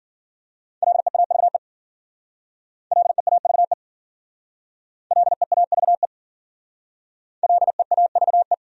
Morse code, heard as a keyed tone, about 53 wpm, 700 Hz. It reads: LEAVE LEAVE LEAVE LEAVE